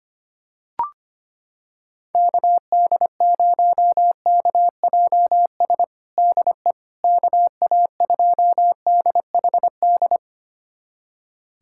Morse code, heard as a keyed tone, about 25 wpm, 700 Hz. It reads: KD0KJS DE KA2DHD
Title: USA Callsign DE Callsign - 6 Characters - 25wpm